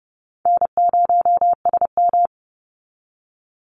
Morse code, heard as a keyed tone, about 29 wpm, 700 Hz. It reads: N0SM
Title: Call Signs - 30wpm